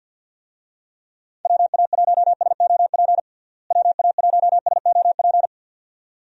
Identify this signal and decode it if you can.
Morse code, heard as a keyed tone, about 52 wpm, 700 Hz. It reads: WA1SOP WA1SOP
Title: USA Call Signs - 6 Characters - Repeated 2x - 50wpm